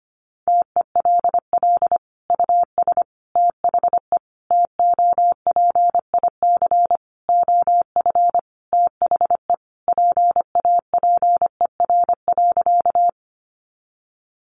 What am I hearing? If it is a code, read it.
Morse code, heard as a keyed tone, about 25 wpm, 700 Hz. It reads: TELL US THE TOPIC OF THE PAPER.
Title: Sentences from Top 1200 Words 25wpm